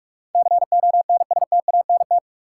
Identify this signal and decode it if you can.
Morse code, heard as a keyed tone, about 45 wpm, 700 Hz. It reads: CONSTANT
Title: Sentences from Top 1500 Words - New Word Drill - 45wpm